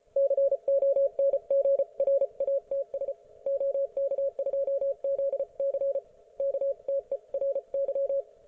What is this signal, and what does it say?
Morse code, heard as a keyed tone, about 34 wpm, 550 Hz. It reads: CONGRATS OK2ZC KTERY